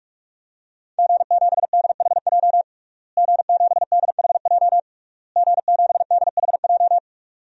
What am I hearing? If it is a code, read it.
Morse code, heard as a keyed tone, about 41 wpm, 700 Hz. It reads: G7DHJ G7DHJ G7DHJ
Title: DXCC-100 Call Signs - 5 Characters - Repeated 3x - 45wpm